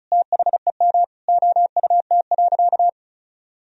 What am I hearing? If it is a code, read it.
Morse code, heard as a keyed tone, about 36 wpm, 700 Hz. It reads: THEM OUT.
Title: Sentences from Top 500 Words and Abbreviations 35wpm